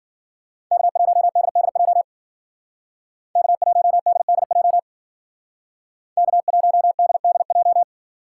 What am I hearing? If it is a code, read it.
Morse code, heard as a keyed tone, about 51 wpm, 700 Hz. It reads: K1DDJ K1DDJ K1DDJ